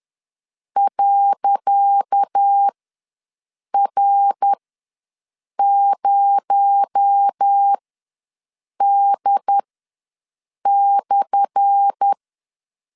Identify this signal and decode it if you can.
Morse code, heard as a keyed tone, about 10 wpm, spaced as for 6 wpm, 800 Hz. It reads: .R0D/
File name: LEZIONI / 10 SIGNS / 10C DOT ws 10wpm 4213